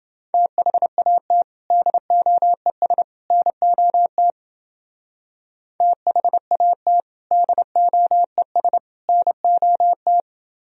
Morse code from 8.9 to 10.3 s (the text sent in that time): NOT